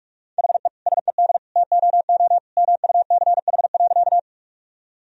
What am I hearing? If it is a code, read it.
Morse code, heard as a keyed tone, about 46 wpm, 700 Hz. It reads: HE SED TOO MUCH.
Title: Sentences from Top 200 Words and Abbreviations 45wpm